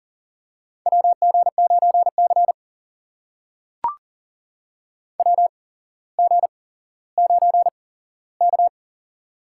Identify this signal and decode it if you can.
Morse code, heard as a keyed tone, about 40 wpm, spaced as for 17 wpm, 700 Hz. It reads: WG9C WG9K
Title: USA Call Signs - 4 Characters - 8x Character Spacing - 40wpm